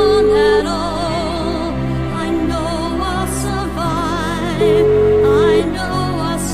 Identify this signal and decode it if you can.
Ringback tone (Europe, 425 Hz), with noise.